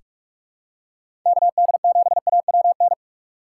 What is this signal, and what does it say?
Morse code, heard as a keyed tone, about 46 wpm, 700 Hz. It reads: K D 7 A W N
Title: USA Call Signs - 6 Characters - 6x Character Spacing - 45wpm